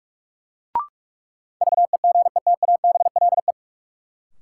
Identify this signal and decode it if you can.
Morse code, heard as a keyed tone, about 43 wpm, 700 Hz. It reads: VEGETABLE
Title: Sentences from Top 1300 Words - New Word Drill - 45wpm